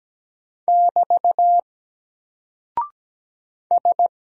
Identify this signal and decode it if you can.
Morse code, heard as a keyed tone, about 17 wpm, 700 Hz. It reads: <BT> S
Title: Single Character (Letter:Number:Punctuation:Prosign) 17wpm